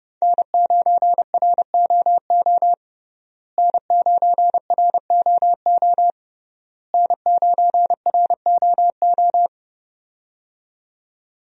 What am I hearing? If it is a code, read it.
Morse code, heard as a keyed tone, about 30 wpm, 700 Hz. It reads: N9ROO N9ROO N9ROO
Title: USA Call Signs - 5 Characters - Repeated 3x - 30wpm